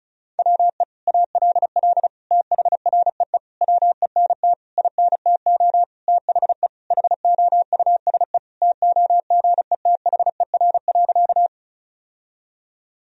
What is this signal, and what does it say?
Morse code, heard as a keyed tone, about 36 wpm, 700 Hz. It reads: WE ALL THREE WENT INTO THE HOUSE TOGETHER.
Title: Sentences from Top 1000 Words - Review Part 1 - 35wpm